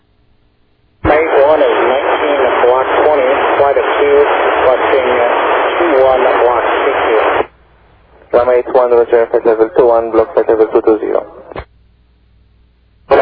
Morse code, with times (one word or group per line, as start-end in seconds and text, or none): none